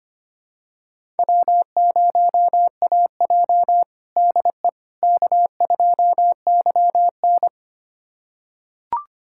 Morse code, heard as a keyed tone, about 25 wpm, 700 Hz. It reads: W0AJ DE K2YN
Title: USA Callsign DE Callsign - 4 Characters - 25wpm